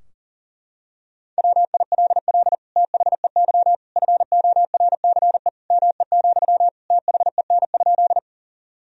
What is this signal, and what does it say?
Morse code, heard as a keyed tone, about 40 wpm, 700 Hz. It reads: WILL THEY FORCE ME, THEN?
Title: Sentences from Top 1000 Words - Review Part 3 - 40wpm